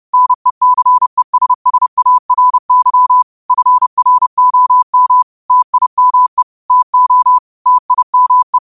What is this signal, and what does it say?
Morse code, heard as a keyed tone, about 30 wpm, 1000 Hz. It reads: NECESSARY FROM TIME TO TIME
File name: qotdfiles / 30wpm